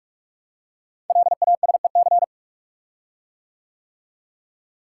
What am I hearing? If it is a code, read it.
Morse code, heard as a keyed tone, about 44 wpm, 690 Hz. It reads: LASEC
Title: Code Groups - 5 Letters - 45wpm